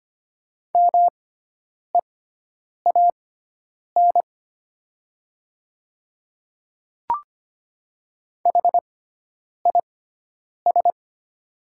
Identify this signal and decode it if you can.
Morse code, heard as a keyed tone, about 25 wpm, spaced as for 9 wpm, 700 Hz. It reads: MEAN HIS
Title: 6x Character Spacing - Top 200 Words - 25wpm